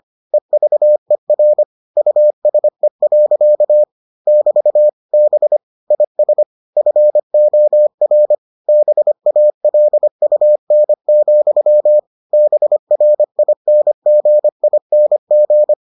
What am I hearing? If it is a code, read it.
Morse code, heard as a keyed tone, about 25 wpm, 600 Hz. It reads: EVER USE. <BT> B IS FOR BALUN, BRINGING